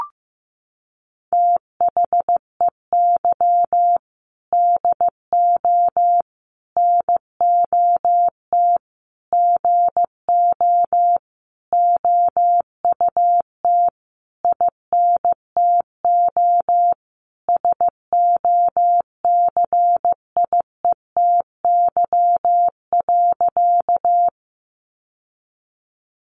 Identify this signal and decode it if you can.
Morse code, heard as a keyed tone, about 15 wpm, 700 Hz. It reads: THEY DO NOT GO OUT INTO SOCIETY.
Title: Sentences from Top 700 Words 15wpm